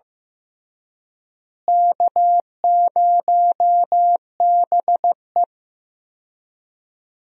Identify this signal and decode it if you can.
Morse code, heard as a keyed tone, about 16 wpm, 700 Hz. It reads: K0BE
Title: USA Call Signs - 4 Characters - Repeated 3x - 15wpm